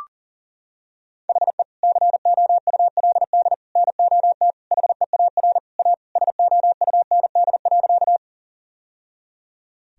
Morse code, heard as a keyed tone, about 40 wpm, 700 Hz. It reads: HE COULD NOT HEAR A SOUND.
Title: Sentences from Top 1000 Words - Review Part 1 - 40wpm